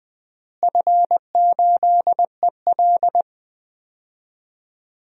Morse code, heard as a keyed tone, about 20 wpm, 700 Hz. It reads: F8EL
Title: DXCC-100 Call Signs - 4 Characters - Repeated 3x - 20wpm